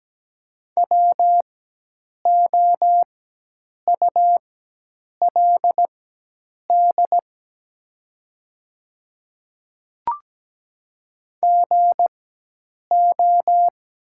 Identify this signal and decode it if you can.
Morse code, heard as a keyed tone, about 17 wpm, spaced as for 8 wpm, 700 Hz. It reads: WOULD GO
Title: 4x Character Spacing - Top 100 Words - 17wpm